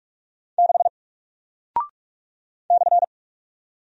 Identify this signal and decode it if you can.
Morse code, heard as a keyed tone, about 45 wpm, 700 Hz. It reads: 6 /
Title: Single Character (Letter:Number:Punctuation:Prosign) 45wpm